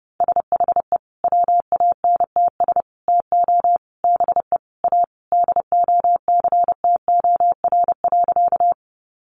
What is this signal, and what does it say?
Morse code, heard as a keyed tone, about 30 wpm, 700 Hz. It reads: SHE WANTS TO BE A DOCTOR.
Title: Easy Phrases 30wpm